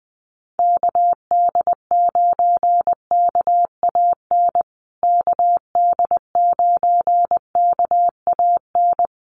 Morse code, heard as a keyed tone, about 20 wpm, 700 Hz. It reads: KD9KAN KD9KAN
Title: USA Call Signs Repeated 3x 20wpm